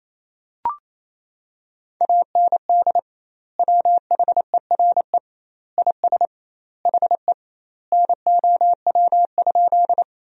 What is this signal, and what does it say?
Morse code, heard as a keyed tone, about 28 wpm, 700 Hz. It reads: AND WHERE IS HE NOW?
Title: 2x Word-Spacing - Sentences from Top 200 Words - 28wpm